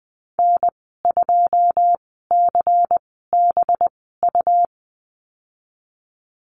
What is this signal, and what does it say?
Morse code, heard as a keyed tone, about 20 wpm, 700 Hz. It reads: N 2 C B U
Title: 231 Lesson 37g - Letter x - Call Signs 20wpm